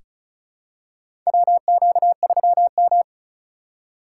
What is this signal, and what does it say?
Morse code, heard as a keyed tone, about 34 wpm, 700 Hz. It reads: WQ3M
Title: USA Call Signs - 4 Characters - 4x Character Spacing - 35wpm